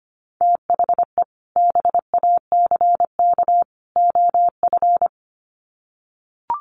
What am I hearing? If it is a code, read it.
Morse code, heard as a keyed tone, about 25 wpm, 700 Hz. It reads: THE BACK OF
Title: Sets of 3 Words 25wpm